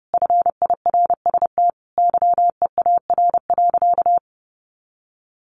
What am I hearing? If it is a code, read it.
Morse code, heard as a keyed tone, about 30 wpm, 700 Hz. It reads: FIRST YEAR.
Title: Sentences from Top 1300 Words 30wpm